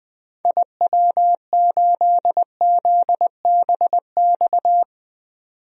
Morse code, heard as a keyed tone, about 21 wpm, 700 Hz. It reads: IW8ZBX
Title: DXCC-100 Call Signs - 6 Characters - 20wpm